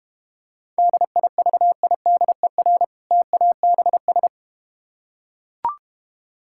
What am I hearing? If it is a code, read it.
Morse code, heard as a keyed tone, about 32 wpm, 700 Hz. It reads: DIVIDER TABS